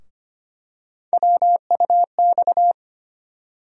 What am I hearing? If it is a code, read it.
Morse code, heard as a keyed tone, about 25 wpm, 700 Hz. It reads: WUX
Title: USA Call Sign Postfixes - 3 Letters 25wpm